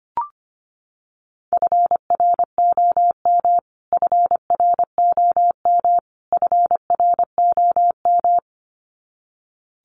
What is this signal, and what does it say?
Morse code, heard as a keyed tone, about 25 wpm, 700 Hz. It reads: FROM FROM FROM
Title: Top 300 Words Repeated 25wpm